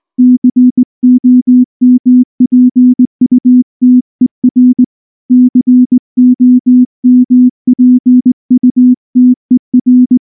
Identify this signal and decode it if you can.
Morse code, heard as a keyed tone, about 18 wpm, 250 Hz. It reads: COMPUTER COMPUTER
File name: morse / morse-training-20WPM-4